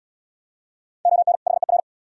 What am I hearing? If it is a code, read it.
Morse code, heard as a keyed tone, about 66 wpm, 690 Hz. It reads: ON HER